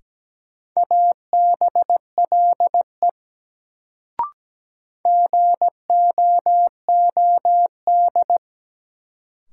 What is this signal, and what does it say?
Morse code, heard as a keyed tone, about 17 wpm, 700 Hz. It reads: ABLE GOOD
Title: Sets of 2 Words - New Word Drill - 17wpm